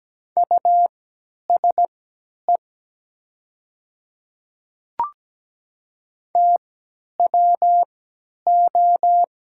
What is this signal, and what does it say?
Morse code, heard as a keyed tone, about 17 wpm, spaced as for 10 wpm, 700 Hz. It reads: USE TWO
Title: 3x Character Spacing - Top 100 Words - 17wpm